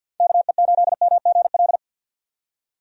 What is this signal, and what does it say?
Morse code, heard as a keyed tone, about 49 wpm, 700 Hz. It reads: KE8MGL
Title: USA Call Signs - 6 Characters - 50wpm